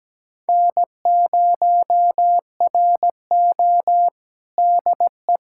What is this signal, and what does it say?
Morse code, heard as a keyed tone, about 17 wpm, 700 Hz. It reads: N0RO DE
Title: USA Callsign DE Callsign 17wpm